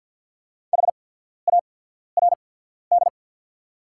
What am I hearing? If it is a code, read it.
Morse code, heard as a keyed tone, about 48 wpm, spaced as for 14 wpm, 690 Hz. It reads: HARD
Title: 8x Character Spacing - Top 500 Words - 50wpm